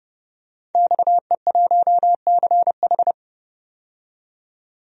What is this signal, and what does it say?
Morse code, heard as a keyed tone, about 30 wpm, 690 Hz. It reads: XE1CH
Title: DXCC-100 Call Signs - 5 Characters - 30wpm